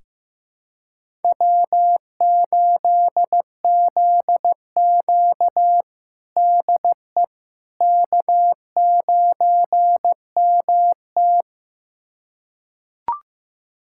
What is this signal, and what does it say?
Morse code, heard as a keyed tone, about 15 wpm, 700 Hz. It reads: W8ZQ DE K9MT
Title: USA Callsign DE Callsign - 4 Characters - 15wpm